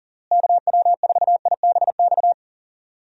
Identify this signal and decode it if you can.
Morse code, heard as a keyed tone, about 40 wpm, 700 Hz. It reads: KW4IBX
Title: USA Call Signs - 6 Characters - Repeated 3x - 40wpm